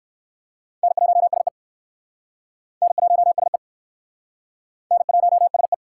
Morse code, heard as a keyed tone, about 59 wpm, 700 Hz. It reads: N1SE N1SE N1SE